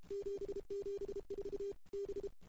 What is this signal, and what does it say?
Morse code, heard as a keyed tone, about 28 wpm, 410 Hz. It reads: 774 B
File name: CW / 2008-ss-cw-k5tr-op-n5rz -05:00